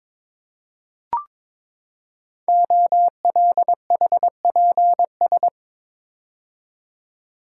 Morse code, heard as a keyed tone, about 22 wpm, 700 Hz. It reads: OLHPS
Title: Code Groups - 5 Letters - 22wpm